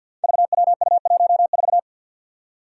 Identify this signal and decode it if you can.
Morse code, heard as a keyed tone, about 53 wpm, 700 Hz. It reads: VWU14
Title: Five Challenging Letter-Number Combinations 50wpm